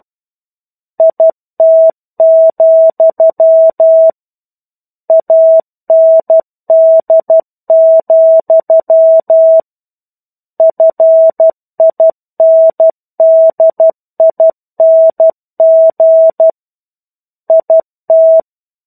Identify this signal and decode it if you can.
Morse code, heard as a keyed tone, about 12 wpm, 650 Hz. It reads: IT, AND, FINDING IT
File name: 01-Chapter-01 12 WPM